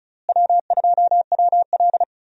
Morse code, heard as a keyed tone, about 34 wpm, 700 Hz. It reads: W2WL
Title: USA Call Signs - 4 Characters - Repeated 3x - 35wpm